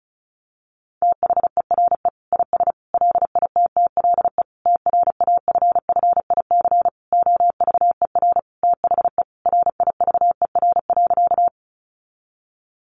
Morse code, heard as a keyed tone, about 34 wpm, 700 Hz. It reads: THERE IS LITTLE TRAFFIC OVER THE RIVER.
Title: Sentences from Top 1300 Words 35wpm